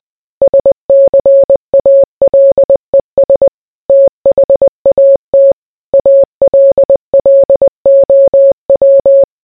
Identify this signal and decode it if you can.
Morse code, heard as a keyed tone, about 20 wpm, 550 Hz. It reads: SCALES THAT ALLOW